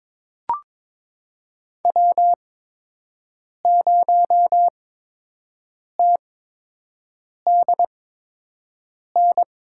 Morse code, heard as a keyed tone, about 22 wpm, spaced as for 6 wpm, 700 Hz. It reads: W0TDN
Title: USA Call Signs - 5 Characters - 8x Character Spacing - 22wpm